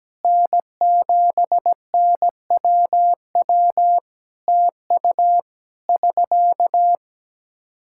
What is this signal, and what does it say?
Morse code, heard as a keyed tone, about 17 wpm, 700 Hz. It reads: N7NWW TU <SK>
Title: Practice QSO - Exchange 4 - Ending - 17wpm